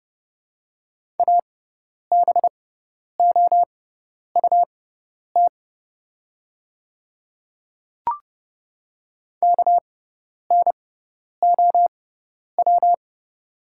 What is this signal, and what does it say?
Morse code, heard as a keyed tone, about 30 wpm, spaced as for 10 wpm, 700 Hz. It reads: ABOUT KNOW
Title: 6x Character Spacing - Top 100 Words - 30wpm